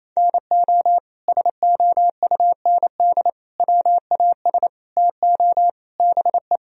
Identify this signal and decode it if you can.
Morse code, heard as a keyed tone, about 28 wpm, 700 Hz. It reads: NO SOUND WAS TO BE
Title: Sentences from Top 1000 Words - Review Part 1 - 28wpm